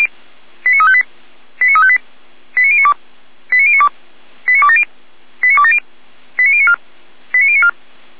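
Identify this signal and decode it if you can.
Five-tone selective call (ZVEI), heard as quick runs of five tones, tones of about 70 ms. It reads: 89358 89358 89092 89092 89270 89270 89094 89094
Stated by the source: ZVEI five-tone call received on SDR.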